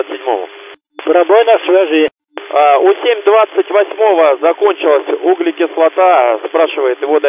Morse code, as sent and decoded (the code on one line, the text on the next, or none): none